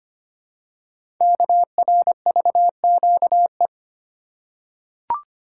Morse code, heard as a keyed tone, about 25 wpm, 700 Hz. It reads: KRVQE
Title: Code Groups - 5 Letters - 25wpm